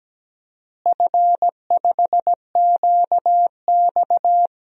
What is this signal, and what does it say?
Morse code, heard as a keyed tone, about 17 wpm, 700 Hz. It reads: F5QX
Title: DXCC-100 Call Signs - 4 Characters - Repeated 2x - 17wpm